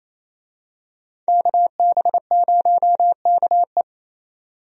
Morse code, heard as a keyed tone, about 28 wpm, 700 Hz. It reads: KB0KE
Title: USA Call Signs - 5 Characters - 28wpm